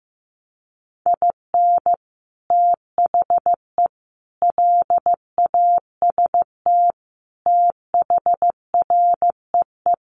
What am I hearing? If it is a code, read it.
Morse code, heard as a keyed tone, about 15 wpm, 700 Hz. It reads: IN THE LAST THREE
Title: Sets of 5 Words Encore2 15wpm